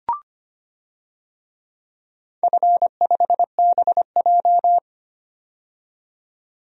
Morse code, heard as a keyed tone, about 25 wpm, 700 Hz. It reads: F5BJ
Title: DXCC-100 Call Signs - 4 Characters - 25wpm